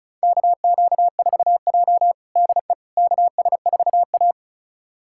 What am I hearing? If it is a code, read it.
Morse code, heard as a keyed tone, about 36 wpm, 700 Hz. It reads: KQ4J DE KS4A
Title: USA Callsign DE Callsign - 4 Characters - 35wpm